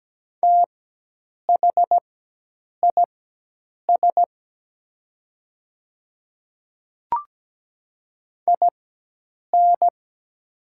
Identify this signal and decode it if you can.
Morse code, heard as a keyed tone, about 17 wpm, spaced as for 8 wpm, 700 Hz. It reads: THIS IN
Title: 4x Character Spacing - Top 300 Words - 17wpm